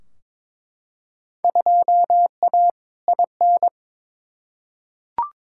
Morse code, heard as a keyed tone, about 22 wpm, 700 Hz. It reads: 2A IN
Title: Field Day Exchanges - 22wpm